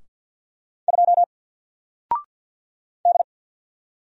Morse code, heard as a keyed tone, about 48 wpm, 700 Hz. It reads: 2 D
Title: Single Character (Letter:Number:Punctuation:Prosign) Alt 50wpm